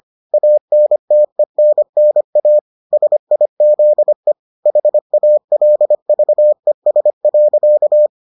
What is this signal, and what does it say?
Morse code, heard as a keyed tone, about 25 wpm, 600 Hz. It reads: ANTENNA SIZE HALVES.